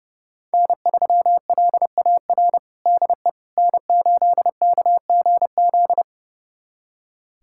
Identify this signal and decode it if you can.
Morse code, heard as a keyed tone, about 30 wpm, 700 Hz. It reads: N3LAR DE N8KGZ
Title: USA Callsign DE Callsign - 5 Characters - 30wpm